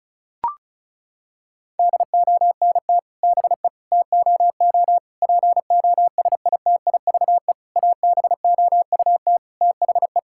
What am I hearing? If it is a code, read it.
Morse code, heard as a keyed tone, about 36 wpm, 700 Hz. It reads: DONT BE TOO POSITIVE ABOUT THE
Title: Sentences from Top 1000 Words - Review Part 5 - 35wpm